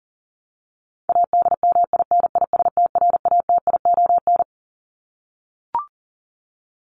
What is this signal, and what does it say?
Morse code, heard as a keyed tone, about 40 wpm, 700 Hz. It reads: ADMINISTRATION
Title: Top 1251-1500 Words 40wpm